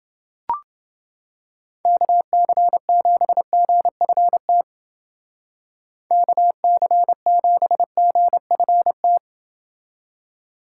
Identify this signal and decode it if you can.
Morse code, heard as a keyed tone, about 28 wpm, 700 Hz. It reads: KC7GFT KC7GFT